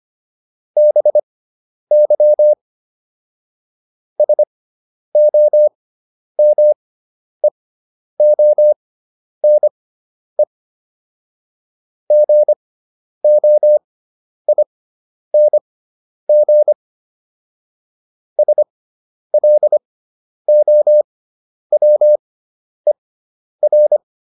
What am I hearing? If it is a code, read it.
Morse code, heard as a keyed tone, about 25 wpm, spaced as for 10 wpm, 600 Hz. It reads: BY SOMEONE GOING SLOWER